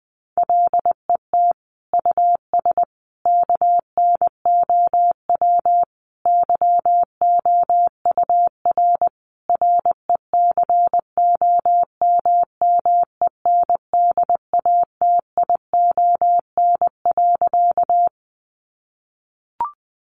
Morse code, heard as a keyed tone, about 20 wpm, 700 Hz. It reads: LET US KNOW YOUR RECOMMENDATION.
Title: Sentences from Top 1400 Words 20wpm